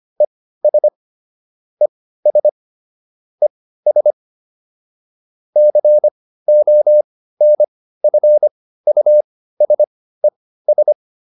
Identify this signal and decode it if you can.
Morse code, heard as a keyed tone, about 25 wpm, spaced as for 15 wpm, 600 Hz. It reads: ES ES ES CONFUSES